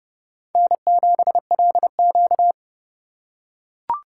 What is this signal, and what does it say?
Morse code, heard as a keyed tone, about 30 wpm, 700 Hz. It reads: N7LQ
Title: USA Call Signs - 4 Characters - 4x Character Spacing - 30wpm